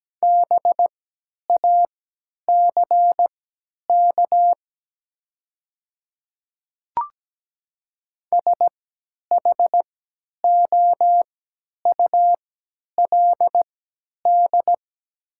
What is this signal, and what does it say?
Morse code, heard as a keyed tone, about 17 wpm, spaced as for 10 wpm, 700 Hz. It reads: BACK SHOULD
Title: 3x Character Spacing - Top 500 Words - 17wpm